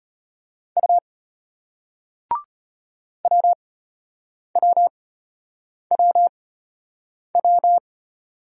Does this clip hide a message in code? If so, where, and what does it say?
Yes: Morse code from 0.8 to 7.8 s, heard as a keyed tone, about 33 wpm, spaced as for 8 wpm, 700 Hz. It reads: U WWWW